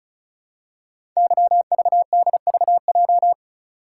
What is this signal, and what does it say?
Morse code, heard as a keyed tone, about 36 wpm, 690 Hz. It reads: YVDVJ
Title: Code Groups - 5 Letters - 35wpm